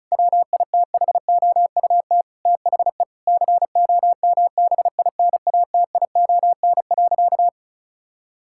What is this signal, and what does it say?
Morse code, heard as a keyed tone, about 34 wpm, 700 Hz. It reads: WITHOUT THE COMBINATION.
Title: Sentences from Top 1000 Words - Review Part 6 - 35wpm